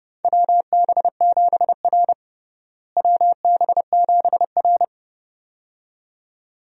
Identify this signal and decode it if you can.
Morse code, heard as a keyed tone, about 32 wpm, 700 Hz. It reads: WB7R WB7R
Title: USA Call Signs - 4 Characters - Repeated 2x - 30wpm